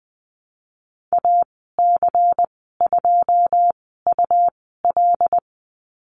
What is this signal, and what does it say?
Morse code, heard as a keyed tone, about 20 wpm, spaced as for 15 wpm, 700 Hz. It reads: AC2UL
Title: 128 Lesson 21g - Number 6 - Call Signs 20wpm